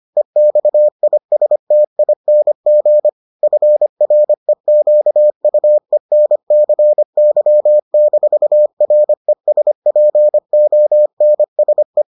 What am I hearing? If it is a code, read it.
Morse code, heard as a keyed tone, about 25 wpm, 600 Hz. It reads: EXISTING FREQUENCY-RESPONSE